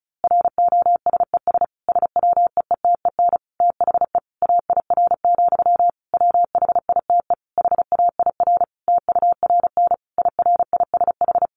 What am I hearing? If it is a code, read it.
Morse code, heard as a keyed tone, about 34 wpm, 700 Hz. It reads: ROSES SWEETEN THE AIR, WHITE HAIR TURN IRISH
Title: A Sound Of Thunder 35wpm